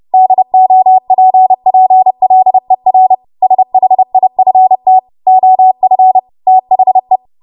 Morse code, heard as a keyed tone, about 30 wpm, 750 Hz. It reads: DOPPLER SHIFT OF THE
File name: Morse / 30WPM